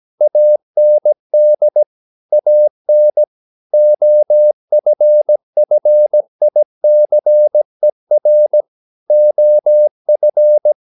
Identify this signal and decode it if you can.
Morse code, heard as a keyed tone, about 17 wpm, 600 Hz. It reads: AND AN OFFICER OF
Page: Romanen Never af Ken Follet på engelsk.